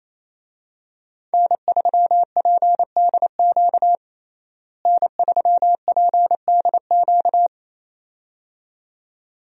Morse code, heard as a keyed tone, about 27 wpm, 700 Hz. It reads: N3PDQ N3PDQ
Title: USA Call Signs - 5 Characters - Repeated 2x - 28wpm